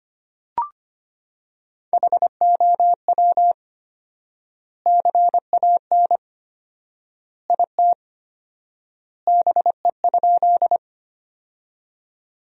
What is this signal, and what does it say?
Morse code, heard as a keyed tone, about 25 wpm, 700 Hz. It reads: HOW CAN IT BE?
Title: 4x Word-Spacing - Sentences from Top 100 Words - 25wpm